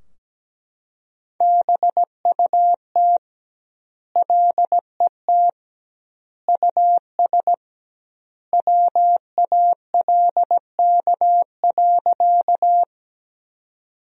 Morse code, heard as a keyed tone, about 17 wpm, 700 Hz. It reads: BUT LET US WALK.
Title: 2x Word-Spacing - Sentences from Top 300 Words - 17wpm